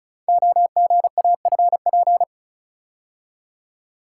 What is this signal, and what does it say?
Morse code, heard as a keyed tone, about 36 wpm, 700 Hz. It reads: OGAFP
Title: Code Groups - 5 Letters - 35wpm